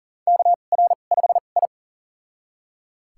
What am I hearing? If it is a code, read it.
Morse code, heard as a keyed tone, about 40 wpm, spaced as for 29 wpm, 690 Hz. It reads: KR5I
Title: USA Call Signs - 2x Character Spacing - 40wpm